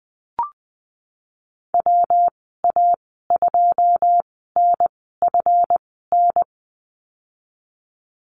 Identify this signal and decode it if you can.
Morse code, heard as a keyed tone, about 20 wpm, spaced as for 15 wpm, 700 Hz. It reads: WA2NFN
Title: 154 Lesson 25g - Letter p - Call Signs 20wpm